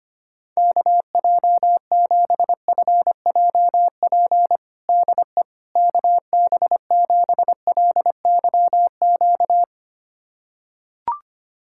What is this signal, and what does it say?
Morse code, heard as a keyed tone, about 25 wpm, 700 Hz. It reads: KJ7FJP DE KB7LYQ
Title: USA Callsign DE Callsign - 6 Characters - 25wpm